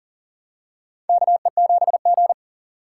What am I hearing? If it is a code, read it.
Morse code, heard as a keyed tone, about 40 wpm, 700 Hz. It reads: KE7G
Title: USA Call Signs - 6x Character Spacing - 40wpm